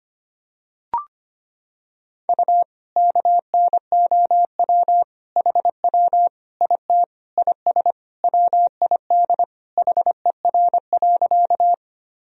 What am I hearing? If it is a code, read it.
Morse code, heard as a keyed tone, about 25 wpm, 700 Hz. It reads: U KNOW HW IT IS WID HER.
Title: Sentences from Top 100 Words and Abbreviations 25wpm